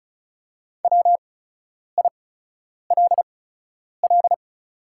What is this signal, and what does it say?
Morse code, heard as a keyed tone, about 36 wpm, spaced as for 10 wpm, 700 Hz. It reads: WILL
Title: 8x Character Spacing - Top 300 Words - 35wpm